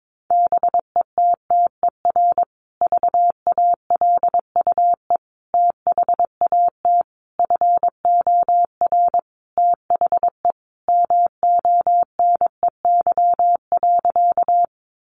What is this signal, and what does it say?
Morse code, heard as a keyed tone, about 22 wpm, 700 Hz. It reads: BETTER VALUE THAT FOR THE MONEY.
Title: Sentences from Top 600 Words 22wpm